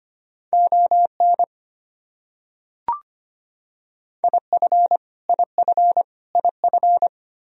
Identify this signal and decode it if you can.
Morse code, heard as a keyed tone, about 25 wpm, 700 Hz. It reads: ON IF IF IF
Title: Top 100 Words Repeated 25wpm